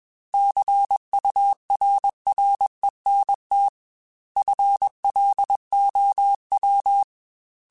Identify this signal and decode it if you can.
Morse code, heard as a keyed tone, about 21 wpm, 800 Hz. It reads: CURRENT FLOW
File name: Technical Subjects / Technical Subjects-mp3s / XL @20